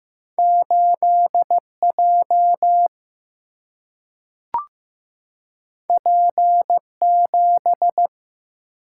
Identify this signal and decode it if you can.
Morse code, heard as a keyed tone, about 15 wpm, 700 Hz. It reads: 8 J P 7
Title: Single-Letter Single-Number 15wpm